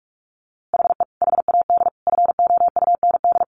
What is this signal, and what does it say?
Morse code, heard as a keyed tone, about 44 wpm, 700 Hz. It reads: HE HAD FOUND
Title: A Sound Of Thunder 45wpm